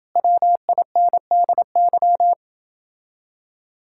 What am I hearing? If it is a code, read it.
Morse code, heard as a keyed tone, about 29 wpm, 700 Hz. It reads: WINDY